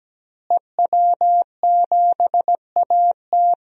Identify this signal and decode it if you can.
Morse code, heard as a keyed tone, about 17 wpm, 700 Hz. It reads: EW7AT
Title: DXCC-100 Call Signs - 5 Characters - Repeated 3x - 17wpm